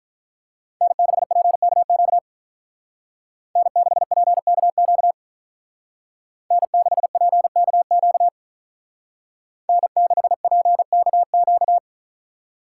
Morse code, heard as a keyed tone, about 44 wpm, 700 Hz. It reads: N6PKQ N6PKQ N6PKQ N6PKQ